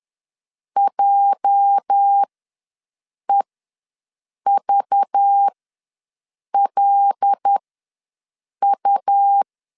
Morse code, heard as a keyed tone, about 10 wpm, spaced as for 6 wpm, 800 Hz. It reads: JEVLU
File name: LEZIONI / 8 YJ45 / 8 YJ45 ws 10wpm 3843